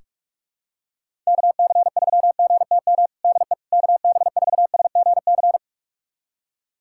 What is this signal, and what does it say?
Morse code, heard as a keyed tone, about 46 wpm, 700 Hz. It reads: KC3GTM DE KB4SGC
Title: USA Callsign DE Callsign - 6 Characters - 45wpm